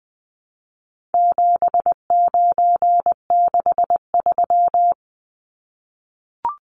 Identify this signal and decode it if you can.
Morse code, heard as a keyed tone, about 20 wpm, 700 Hz. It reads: 7963
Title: Numbers - 4-Digit 20wpm